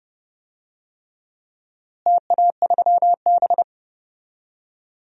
Morse code, heard as a keyed tone, about 32 wpm, 700 Hz. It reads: TA3B
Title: DXCC-100 Call Signs - 4 Characters - Repeated 3x - 30wpm